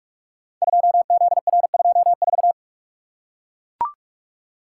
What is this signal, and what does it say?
Morse code, heard as a keyed tone, about 44 wpm, 700 Hz. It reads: 2ZR24
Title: Code Groups - 5 Letter:Numbers - 45wpm